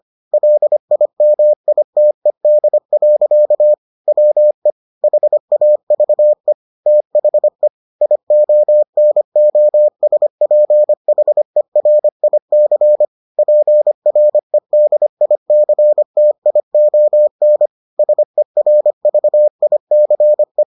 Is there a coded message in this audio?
Yes: Morse code from 0.3 to 20.6 s, heard as a keyed tone, about 25 wpm, 600 Hz. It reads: LIMITED. WE HAVE THE IONOSPHERIC PREDICTION SERVICE